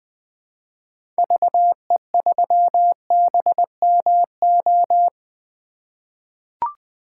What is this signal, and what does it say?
Morse code, heard as a keyed tone, about 21 wpm, 700 Hz. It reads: VE3BMO
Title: 2023 POTA Activator Call Signs - 20wpm